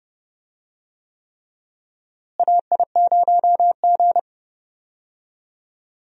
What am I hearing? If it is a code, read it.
Morse code, heard as a keyed tone, about 30 wpm, 700 Hz. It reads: AI0G
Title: USA Call Signs - 4 Characters - Repeated 3x - 30wpm